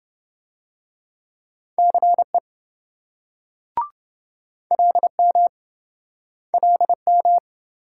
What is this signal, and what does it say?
Morse code, heard as a keyed tone, about 27 wpm, 700 Hz. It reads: CE LM LM